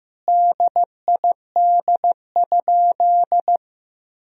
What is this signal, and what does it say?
Morse code, heard as a keyed tone, about 15 wpm, 700 Hz. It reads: DID?
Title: Sentences from Top 100 Words and Abbreviations 15wpm